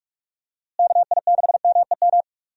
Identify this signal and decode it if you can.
Morse code, heard as a keyed tone, about 44 wpm, 700 Hz. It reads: KI6MEM
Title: USA Call Signs - 6 Characters - Repeated 3x - 45wpm